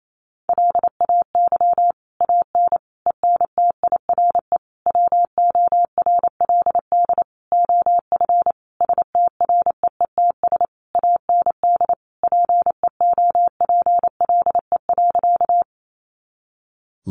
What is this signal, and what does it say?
Morse code, heard as a keyed tone, about 28 wpm, 700 Hz. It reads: LAY AN ENTIRE WORLD OF STREETS AND PEOPLE.
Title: A Sound Of Thunder 28wpm